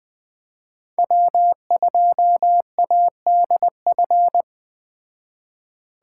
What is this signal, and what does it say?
Morse code, heard as a keyed tone, about 21 wpm, 700 Hz. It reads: W2ADF